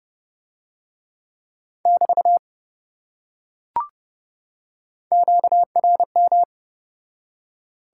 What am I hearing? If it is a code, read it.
Morse code, heard as a keyed tone, about 26 wpm, 700 Hz. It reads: <BT> QRM